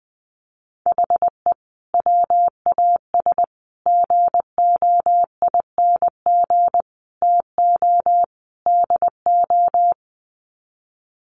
Morse code, heard as a keyed tone, about 20 wpm, 700 Hz. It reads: HE WAS GOING TO DO
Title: Sets of 5 Words Encore2 20wpm